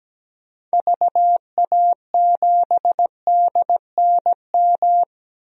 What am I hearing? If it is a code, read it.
Morse code, heard as a keyed tone, about 17 wpm, 700 Hz. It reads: VA7DNM
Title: DXCC-100 Call Signs - 6 Characters - 17wpm